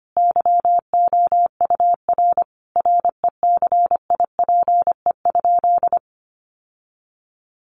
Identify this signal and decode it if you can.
Morse code, heard as a keyed tone, about 25 wpm, 700 Hz. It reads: YOUR RECIPE?
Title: Sentences from Top 1200 Words 25wpm